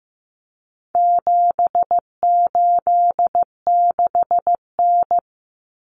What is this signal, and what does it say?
Morse code, heard as a keyed tone, about 15 wpm, 700 Hz. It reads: 786N
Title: Common Cut Numbers - 4-Digit 15wpm